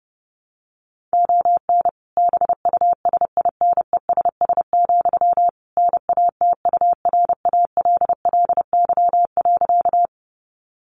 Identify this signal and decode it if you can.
Morse code, heard as a keyed tone, about 30 wpm, 700 Hz. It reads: ON BUSINESS, NATURALLY.
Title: Sentences from Top 1300 Words 30wpm